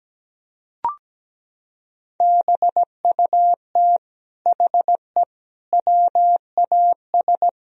Morse code, heard as a keyed tone, about 17 wpm, 700 Hz. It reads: BUT HE WAS
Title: Sentences from Top 1000 Words - Review Part 3 - 17wpm